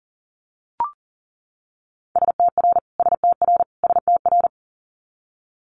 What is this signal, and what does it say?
Morse code, heard as a keyed tone, about 38 wpm, 700 Hz. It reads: STR STR STR
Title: Multi-letter Phonemes Repeated 3x 40wpm